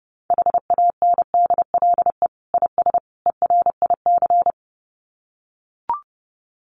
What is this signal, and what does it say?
Morse code, heard as a keyed tone, about 30 wpm, 700 Hz. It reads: HANDLE IS ERIC
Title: Common QSO Phrases 30wpm